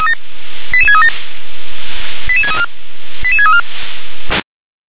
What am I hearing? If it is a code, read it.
Five-tone selective call (ZVEI), heard as quick runs of five tones, tones of about 70 ms. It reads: 88637 88634 88634
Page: SDR/ZVEI